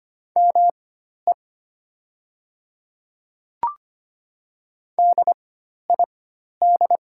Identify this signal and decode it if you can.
Morse code, heard as a keyed tone, about 25 wpm, spaced as for 12 wpm, 700 Hz. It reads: ME DID
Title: 4x Character Spacing - Top 200 Words - 25wpm